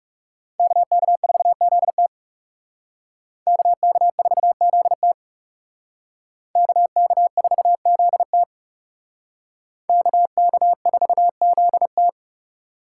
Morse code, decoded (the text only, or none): KK4ZT KK4ZT KK4ZT KK4ZT